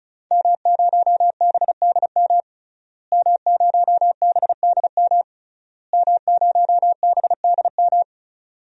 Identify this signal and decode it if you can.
Morse code, heard as a keyed tone, about 36 wpm, 700 Hz. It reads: M0BDM M0BDM M0BDM
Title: DXCC-100 Call Signs - 5 Characters - Repeated 3x - 35wpm